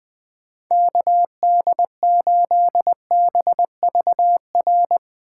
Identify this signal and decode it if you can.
Morse code, heard as a keyed tone, about 22 wpm, 700 Hz. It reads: KD8BVR
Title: Speed Racing - USA 2x3 Call Signs 20wpm